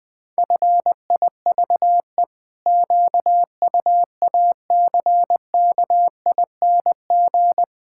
Morse code, heard as a keyed tone, about 19 wpm, 700 Hz. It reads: FIVE QUACKING
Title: Pangram Challenge v2 - 20wpm